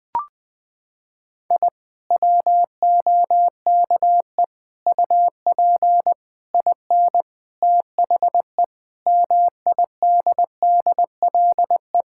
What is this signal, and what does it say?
Morse code, heard as a keyed tone, about 20 wpm, 700 Hz. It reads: I WOKE UP IN THE MIDDLE
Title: Sentences with 7-Word Phrases - 20wpm